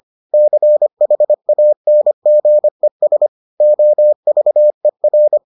Morse code, heard as a keyed tone, about 25 wpm, 600 Hz. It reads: CHANGES OVER